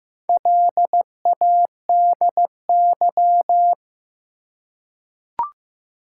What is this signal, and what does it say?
Morse code, heard as a keyed tone, about 15 wpm, 700 Hz. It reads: LADY
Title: Four Letter Words - 15wpm